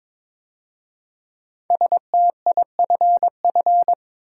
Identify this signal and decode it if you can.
Morse code, heard as a keyed tone, about 22 wpm, 700 Hz. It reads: STIFF
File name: 5 Letter Words - First 3 Letters Given - 22wpm